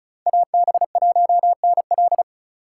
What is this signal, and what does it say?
Morse code, heard as a keyed tone, about 34 wpm, 700 Hz. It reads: AB1NL
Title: USA Call Signs - 3x Character Spacing - 35wpm